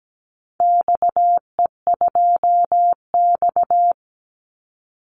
Morse code, heard as a keyed tone, about 17 wpm, 700 Hz. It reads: XE2X
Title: Call Signs Repeated 3x - 17wpm